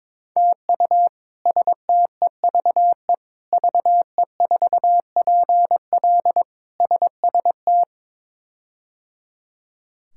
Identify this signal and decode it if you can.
Morse code, heard as a keyed tone, about 22 wpm, 700 Hz. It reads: TU STEVE VE4PL SST
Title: K1USN SST QSO Practice - 22wpm